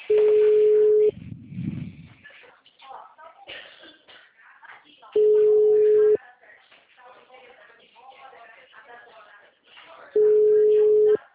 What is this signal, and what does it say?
Ringback tone (Europe, 425 Hz), with speech.
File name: out- -2022- - - .19958